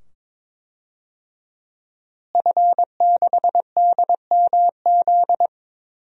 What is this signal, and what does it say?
Morse code, heard as a keyed tone, about 23 wpm, 700 Hz. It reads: F6DMZ
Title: DXCC-100 Call Signs - 5 Characters - Repeated 2x - 22wpm